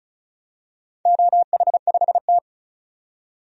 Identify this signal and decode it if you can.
Morse code, heard as a keyed tone, about 34 wpm, 700 Hz. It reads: OH5T
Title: DXCC-100 Call Signs - 4 Characters - 35wpm